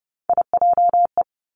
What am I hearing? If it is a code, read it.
Morse code, heard as a keyed tone, about 30 wpm, 700 Hz. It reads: IJE
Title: Three Challenging Letter-Number Combinations 30wpm